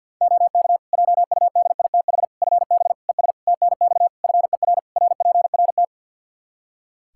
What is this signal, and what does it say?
Morse code, heard as a keyed tone, about 51 wpm, 700 Hz. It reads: OK JUDITH FB ES TNX FER RPRT
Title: Practice QSO - Exchange 2 - Info - 50wpm